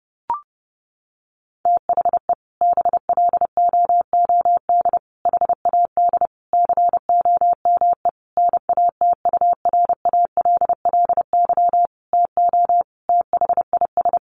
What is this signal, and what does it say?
Morse code, heard as a keyed tone, about 30 wpm, 700 Hz. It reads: THE BLOOD HAD COME NATURALLY TO THIS
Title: Sentences from Top 1300 Words 30wpm